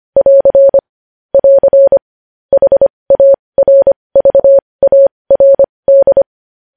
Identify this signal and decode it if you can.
Morse code, heard as a keyed tone, about 25 wpm, spaced as for 20 wpm, 550 Hz. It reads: <AR> <AR> HARVARD
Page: news-headlines-in-morse-code-at-20-wpm